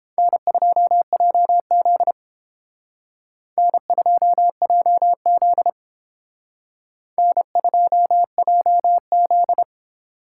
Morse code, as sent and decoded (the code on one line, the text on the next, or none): -. ..--- .--- --.. / -. ..--- .--- --.. / -. ..--- .--- --..
N2JZ N2JZ N2JZ